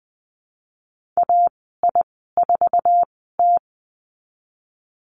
Morse code, heard as a keyed tone, about 20 wpm, spaced as for 15 wpm, 700 Hz. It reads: AI4T
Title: 048 Lesson 9g - Number 4 - Call Signs 20wpm